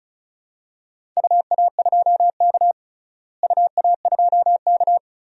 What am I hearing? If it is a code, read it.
Morse code, heard as a keyed tone, about 34 wpm, 700 Hz. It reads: UA2K UA2K
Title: DXCC-100 Call Signs - 4 Characters - Repeated 3x - 35wpm